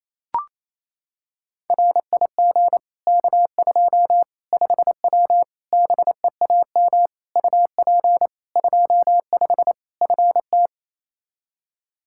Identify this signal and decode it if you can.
Morse code, heard as a keyed tone, about 28 wpm, 700 Hz. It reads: RIG K2 5W BEAM UP 25 FT
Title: Practice QSO - Exchange 2 - Info - 28wpm